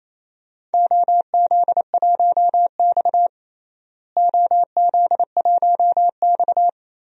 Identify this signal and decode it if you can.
Morse code, heard as a keyed tone, about 28 wpm, 700 Hz. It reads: OZ1X OZ1X
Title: DXCC-100 Call Signs - 4 Characters - Repeated 3x - 28wpm